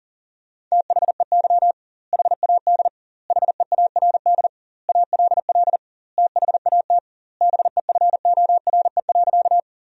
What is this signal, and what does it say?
Morse code, heard as a keyed tone, about 40 wpm, 700 Hz. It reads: THEY HAD HEARD ALL THAT BEFORE.
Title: 2x Word-Spacing - Sentences from Top 400 Words - 40wpm